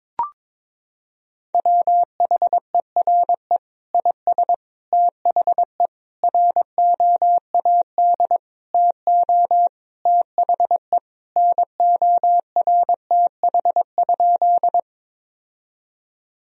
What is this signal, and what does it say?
Morse code, heard as a keyed tone, about 22 wpm, 700 Hz. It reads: WHERE IS THE ROAD TO THE NORTH?
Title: Sentences from Top 1000 Words - Review Part 2 - 22wpm